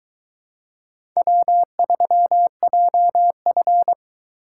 Morse code, heard as a keyed tone, about 22 wpm, 700 Hz. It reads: W3JF